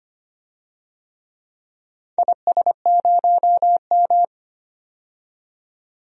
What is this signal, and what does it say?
Morse code, heard as a keyed tone, about 25 wpm, 700 Hz. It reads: IS0M
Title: DXCC-100 Call Signs - 4 Characters - 25wpm